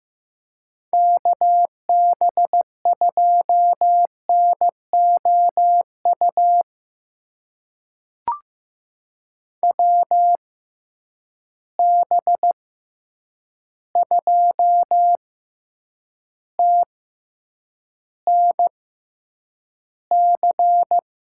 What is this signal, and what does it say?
Morse code, heard as a keyed tone, about 15 wpm, spaced as for 8 wpm, 700 Hz. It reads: KB2NOU WB2TNC
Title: USA Call Signs - 6 Characters - 6x Character Spacing - 15wpm